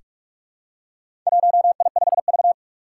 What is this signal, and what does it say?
Morse code, heard as a keyed tone, about 44 wpm, 700 Hz. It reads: 1I5V
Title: Four Challenging Letter-Number Combinations 45wpm